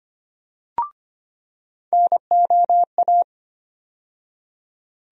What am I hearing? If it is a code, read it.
Morse code, heard as a keyed tone, about 25 wpm, 700 Hz. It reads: NOA
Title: USA Call Sign Postfixes - 3 Letters 25wpm